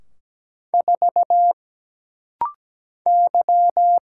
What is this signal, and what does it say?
Morse code, heard as a keyed tone, about 17 wpm, 700 Hz. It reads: 4 Y
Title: Single Letter-Number - Rapid-Fire 17wpm